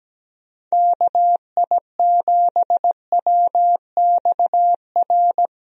Morse code, heard as a keyed tone, about 17 wpm, 700 Hz. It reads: KI7WXR
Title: USA Call Signs - 6 Characters - Repeated 3x - 17wpm